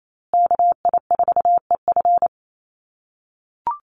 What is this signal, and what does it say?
Morse code, heard as a keyed tone, about 28 wpm, 700 Hz. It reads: KI4EF
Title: USA Call Signs Repeated 2x 28wpm